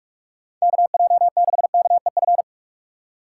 Morse code, heard as a keyed tone, about 46 wpm, 700 Hz. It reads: KJ6KEF
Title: USA Call Signs - 6 Characters - 3x Character Spacing - 45wpm